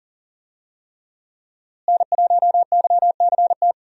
Morse code, heard as a keyed tone, about 37 wpm, 700 Hz. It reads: N1YCT
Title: USA Call Signs - 5 Characters - Repeated 3x - 40wpm